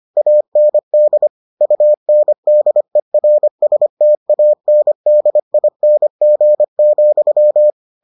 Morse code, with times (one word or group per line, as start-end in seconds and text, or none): 0.2-1.3 AND
1.6-7.7 UNDERSTANDING,